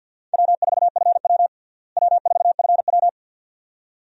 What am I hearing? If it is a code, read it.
Morse code, heard as a keyed tone, about 53 wpm, 690 Hz. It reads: W4FW W4FW
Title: USA Call Signs - 4 Characters - Repeated 2x - 50wpm